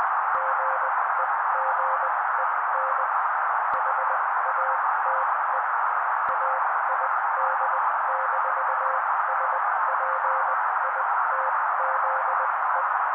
Morse code, heard as a keyed tone, about 20 wpm, spaced as for 16 wpm, 530 Hz. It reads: GEGEN HATE AID-SPITZE
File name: daily / 16wpm noise